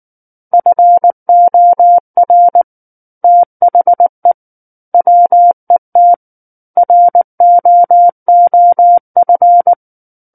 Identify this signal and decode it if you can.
Morse code, heard as a keyed tone, about 19 wpm, 700 Hz. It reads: FOR THE WET ROOF